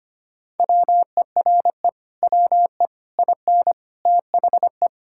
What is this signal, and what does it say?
Morse code, heard as a keyed tone, about 25 wpm, 700 Hz. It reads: WERE WE IN THE
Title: Sentences from Top 1000 Words - Review Part 6 - 25wpm